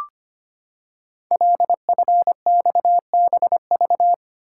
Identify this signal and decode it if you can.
Morse code, heard as a keyed tone, about 25 wpm, 700 Hz. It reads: LFXBV
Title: Code Groups - 5 Letters - 25wpm